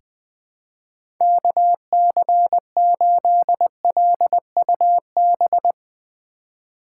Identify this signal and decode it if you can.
Morse code, heard as a keyed tone, about 20 wpm, 700 Hz. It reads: KC8LUB
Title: 2023 POTA Hunter Call Signs - 20wpm